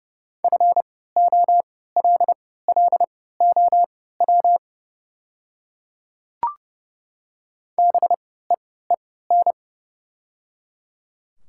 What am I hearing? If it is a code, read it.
Morse code, heard as a keyed tone, about 30 wpm, spaced as for 17 wpm, 700 Hz. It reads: FOLLOW BEEN
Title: 3x Character Spacing - Top 300 Words - 30wpm